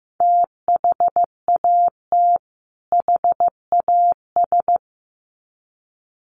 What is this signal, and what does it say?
Morse code, heard as a keyed tone, about 15 wpm, 700 Hz. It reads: THAT HAS
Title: Sets of 2 Words 15wpm